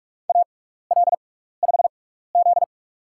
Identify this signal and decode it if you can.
Morse code, heard as a keyed tone, about 46 wpm, spaced as for 15 wpm, 700 Hz. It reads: AL5Z
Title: USA Call Signs - 4 Characters - 6x Character Spacing - 45wpm